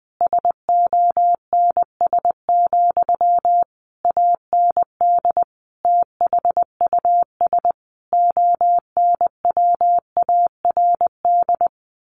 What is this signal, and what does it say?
Morse code, heard as a keyed tone, about 20 wpm, 700 Hz. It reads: SONS, AND THUS ONWARD
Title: A Sound Of Thunder 20wpm